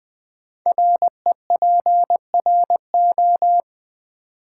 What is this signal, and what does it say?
Morse code, heard as a keyed tone, about 20 wpm, 700 Hz. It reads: REPRO
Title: Build A Word - 5 Letters At A Time - 12-Letter Words - 20wpm